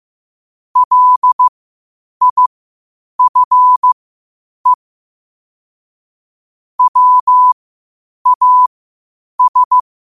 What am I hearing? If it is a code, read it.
Morse code, heard as a keyed tone, about 15 wpm, spaced as for 8 wpm, 1000 Hz. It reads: LIFE WAS